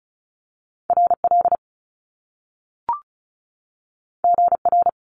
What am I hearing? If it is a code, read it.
Morse code, heard as a keyed tone, about 36 wpm, 700 Hz. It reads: R L G R
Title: Top Two Letter Combinations in Words 35wpm